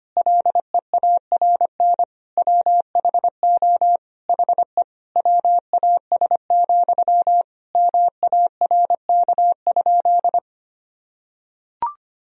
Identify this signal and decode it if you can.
Morse code, heard as a keyed tone, about 25 wpm, 700 Hz. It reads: LEARN WHO HE WAS, MARK?
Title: Sentences from Top 1000 Words - Review Part 1 - 25wpm